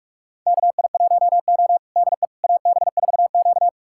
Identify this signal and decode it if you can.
Morse code, heard as a keyed tone, about 46 wpm, 700 Hz. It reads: KI1O DE AB4Q
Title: USA Callsign DE Callsign - 4 Characters - 45wpm